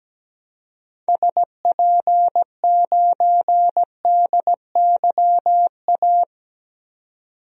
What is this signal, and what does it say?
Morse code, heard as a keyed tone, about 17 wpm, 700 Hz. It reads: SP9DYA
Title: DXCC-100 Call Signs - 6 Characters - 17wpm